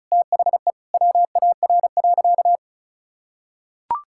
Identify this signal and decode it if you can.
Morse code, heard as a keyed tone, about 36 wpm, 700 Hz. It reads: THE WAR.
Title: Sentences from Top 1000 Words - Review Part 6 - 35wpm